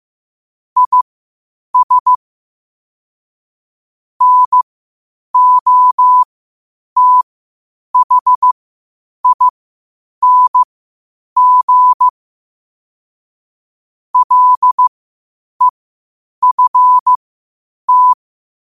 Morse code, heard as a keyed tone, about 15 wpm, spaced as for 8 wpm, 1000 Hz. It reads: IS NOTHING LEFT
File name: qotdfiles / 8wpm